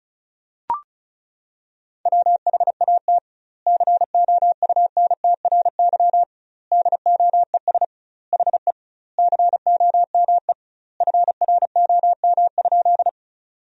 Morse code, heard as a keyed tone, about 34 wpm, 700 Hz. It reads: WHAT COUNTRY DOES HE COME FROM?
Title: 2x Word-Spacing - Sentences from Top 300 Words - 35wpm